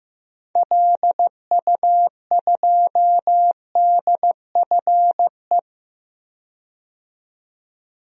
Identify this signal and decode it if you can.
Morse code, heard as a keyed tone, about 15 wpm, 700 Hz. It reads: LU2DFE
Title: DXCC-100 Call Signs - 6 Characters - 15wpm